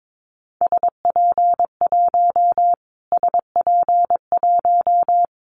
Morse code, heard as a keyed tone, about 22 wpm, 700 Hz. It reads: SP1 SP1
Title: Callsign Prefixes for Top 25 DXCC - Repeated 3x - 22wpm